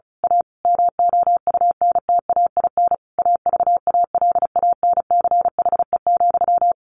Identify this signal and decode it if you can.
Morse code, heard as a keyed tone, about 34 wpm, 700 Hz. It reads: A MOUNTAIN AVALANCHE,
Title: A Sound Of Thunder 35wpm